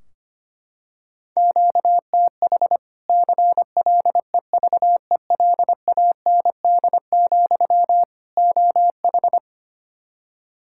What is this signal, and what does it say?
Morse code, heard as a keyed tone, about 25 wpm, 700 Hz. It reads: QTH CLEVELAND, OH
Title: Practice QSO - Exchange 1 - Beginning - 25wpm